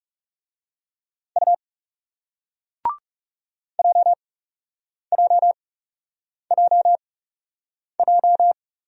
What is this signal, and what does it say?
Morse code, heard as a keyed tone, about 38 wpm, spaced as for 8 wpm, 700 Hz. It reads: U JJJJ